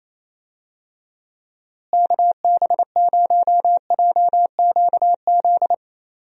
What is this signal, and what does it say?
Morse code, heard as a keyed tone, about 28 wpm, 700 Hz. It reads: KB0JQZ
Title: USA Call Signs - 6 Characters - Repeated 3x - 28wpm